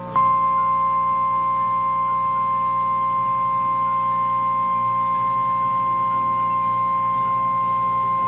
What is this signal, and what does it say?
Emergency Alert System (SAME) tones (the 1050 Hz attention tone). It is heard from 1.0 to 7.9 s.
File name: eas / KNCYKBIE / KNCYKBIE